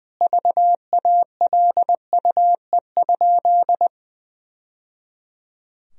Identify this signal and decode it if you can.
Morse code, heard as a keyed tone, about 20 wpm, 700 Hz. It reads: VALUE?
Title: Sentences from Top 1000 Words - Review Part 3 - 20wpm